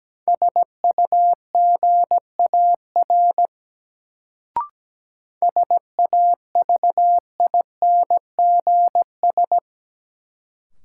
Morse code, heard as a keyed tone, about 17 wpm, 700 Hz. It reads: SUGAR SAVINGS
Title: Sentences from Top 1200 Words - New Word Drill - 17wpm